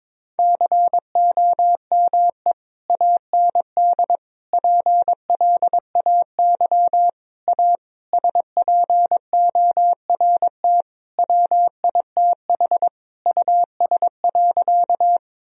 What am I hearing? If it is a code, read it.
Morse code, heard as a keyed tone, about 22 wpm, 700 Hz. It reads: COME AND PLAY A SPORT WITH US.
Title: Sentences from Top 1000 Words - Review Part 4 - 22wpm